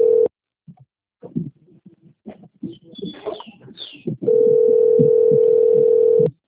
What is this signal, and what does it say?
Ringback tone (North America), 440 + 480 Hz.